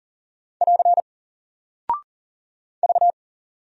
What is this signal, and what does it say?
Morse code, heard as a keyed tone, about 40 wpm, 700 Hz. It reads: <AR> V
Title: Single Character (Letter:Number:Punctuation:Prosign) Alt 40wpm